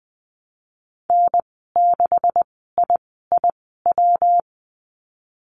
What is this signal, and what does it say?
Morse code, heard as a keyed tone, about 20 wpm, spaced as for 15 wpm, 700 Hz. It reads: N6IIW
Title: 128 Lesson 21g - Number 6 - Call Signs 20wpm